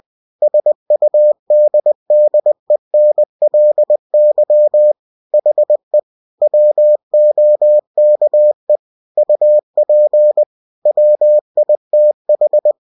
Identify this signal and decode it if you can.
Morse code, heard as a keyed tone, about 20 wpm, 600 Hz. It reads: SUDDENLY HE WOKE UP WITH
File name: cw-books / winnie-the-pooh / winnie-20wpm-page-0094